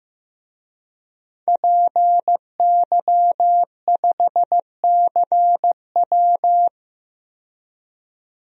Morse code, heard as a keyed tone, about 15 wpm, 700 Hz. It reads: PY5CW
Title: DXCC-100 Call Signs - 5 Characters - 15wpm